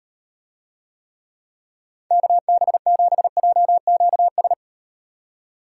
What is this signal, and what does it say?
Morse code, heard as a keyed tone, about 38 wpm, 700 Hz. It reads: KB7JQS